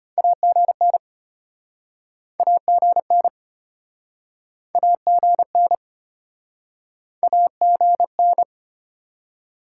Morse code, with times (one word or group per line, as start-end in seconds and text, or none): 0.2-1.0 AGN
2.4-3.3 AGN
4.7-5.8 AGN
7.2-8.4 AGN